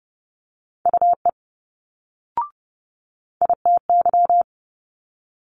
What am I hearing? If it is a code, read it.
Morse code, heard as a keyed tone, about 30 wpm, 700 Hz. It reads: UE ITY
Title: Multi-letter Phonemes Quick Fire 30wpm